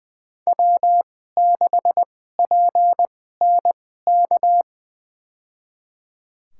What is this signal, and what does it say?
Morse code, heard as a keyed tone, about 20 wpm, spaced as for 14 wpm, 700 Hz. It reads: W6PNK
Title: USA Call Signs - 5 Characters - 2x Character Spacing - 20wpm